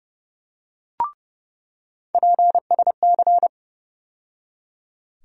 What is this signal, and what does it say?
Morse code, heard as a keyed tone, about 30 wpm, 700 Hz. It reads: PSC
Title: USA Call Sign Postfixes - 3 Letters 30wpm